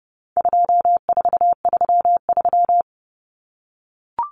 Morse code, heard as a keyed tone, about 28 wpm, 700 Hz. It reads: 2433
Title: Common Cut Numbers - 4-Digit 30wpm